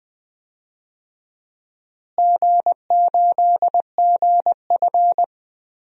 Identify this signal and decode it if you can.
Morse code, heard as a keyed tone, about 21 wpm, 700 Hz. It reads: G8GF
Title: DXCC-100 Call Signs - 4 Characters - Repeated 2x - 20wpm